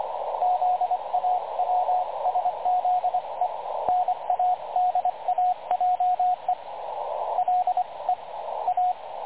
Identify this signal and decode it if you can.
Morse code, heard as a keyed tone, about 24 wpm, 730 Hz, with radio noise. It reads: ZAWSZE NADAJE LE A
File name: dlazaawansowanych / nagraniazpasma / websdr recording 3565.0kHz